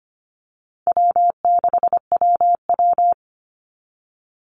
Morse code, heard as a keyed tone, about 25 wpm, 700 Hz. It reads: W6WW
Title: NA SOTA Chaser Callsigns 25wpm